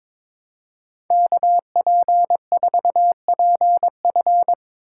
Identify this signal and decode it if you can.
Morse code, heard as a keyed tone, about 25 wpm, 700 Hz. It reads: KP4PF
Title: Speed Racing - USA 2x2 Call Signs 22wpm